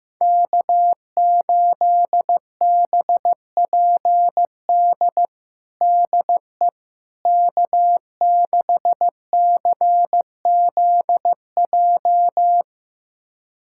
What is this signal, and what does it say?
Morse code, heard as a keyed tone, about 15 wpm, 700 Hz. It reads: K8BPD DE K6CZJ
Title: USA Callsign DE Callsign - 5 Characters - 15wpm